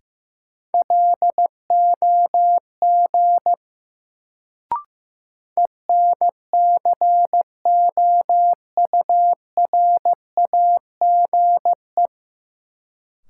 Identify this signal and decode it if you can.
Morse code, heard as a keyed tone, about 15 wpm, 700 Hz. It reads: LOG ENCOURAGE
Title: Sentences from Top 1400 Words - New Word Drill - 15wpm